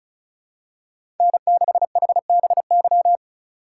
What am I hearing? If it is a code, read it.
Morse code, heard as a keyed tone, about 36 wpm, 700 Hz. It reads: N6HBY
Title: USA Call Signs - 5 Characters - 4x Character Spacing - 35wpm